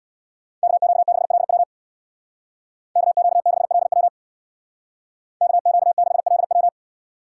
Morse code, heard as a keyed tone, about 65 wpm, 700 Hz. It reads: KQ6LW KQ6LW KQ6LW